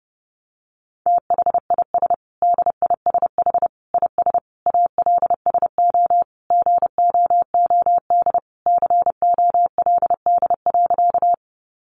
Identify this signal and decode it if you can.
Morse code, heard as a keyed tone, about 30 wpm, 700 Hz. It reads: THIS DISH IS ALSO GOOD COLD.
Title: Sentences from Top 1300 Words 30wpm